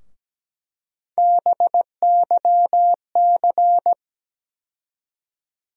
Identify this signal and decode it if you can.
Morse code, heard as a keyed tone, about 17 wpm, 700 Hz. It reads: BYC
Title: USA Call Sign Postfixes - 3 Letters 17wpm